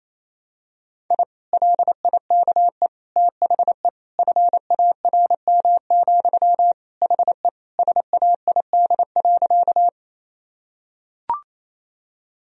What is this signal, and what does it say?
Morse code, heard as a keyed tone, about 28 wpm, 700 Hz. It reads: I LIKE THE FARM, HE SAID.
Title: Sentences from Top 1000 Words - Review Part 2 - 28wpm